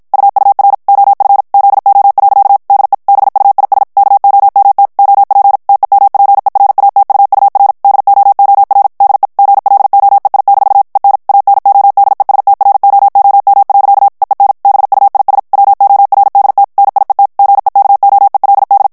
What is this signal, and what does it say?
Morse code, heard as a keyed tone, about 56 wpm, 800 Hz. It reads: UUR OF ZO. DE BUIS KOMT OP TEMPERATUUR DOOR DE GLOEI- EN ANODESTROOM. EEN BUIS WORDT NIET GEKOELD
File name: HOLLANDAIS texte clair 2 vitesse 50 mots minute